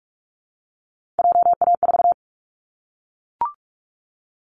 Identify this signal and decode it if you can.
Morse code, heard as a keyed tone, about 44 wpm, 700 Hz. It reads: JA4
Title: Callsign Prefixes for Top 25 DXCC - Repeated 3x - 45wpm